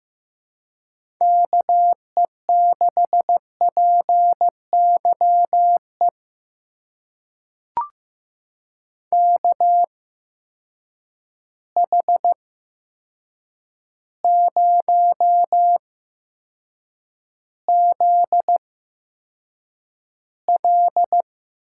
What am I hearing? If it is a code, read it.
Morse code, heard as a keyed tone, about 15 wpm, spaced as for 7 wpm, 700 Hz. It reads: KE6PYE KH0ZL
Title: USA Call Signs - 8x Character Spacing - 15wpm